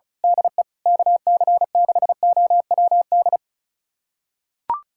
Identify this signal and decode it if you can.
Morse code, heard as a keyed tone, about 36 wpm, 700 Hz. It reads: DE KC6OWD
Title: USA Callsign DE Callsign - 6 Characters - 35wpm